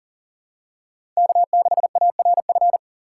Morse code, heard as a keyed tone, about 37 wpm, 700 Hz. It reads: K6ARF
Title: USA Call Signs - 3x Character Spacing - 40wpm